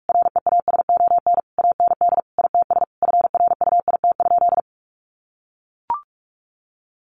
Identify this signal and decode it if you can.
Morse code, heard as a keyed tone, about 44 wpm, 700 Hz. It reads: REASON AND ITS FRUIT?
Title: Sentences from Top 1500 Words 45wpm